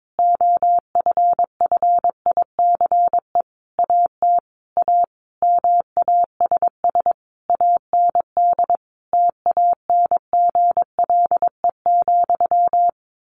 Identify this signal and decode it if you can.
Morse code, heard as a keyed tone, about 22 wpm, 700 Hz. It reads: OFFICE AT A MASS AND TANGLE,
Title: A Sound Of Thunder 22wpm